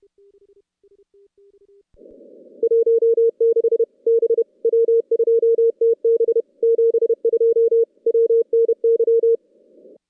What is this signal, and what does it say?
Morse code, heard as a keyed tone, about 30 wpm, 440 Hz. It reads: 16 B W2TB 72 WNY
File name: contests / CW-SS-k5tr- -2100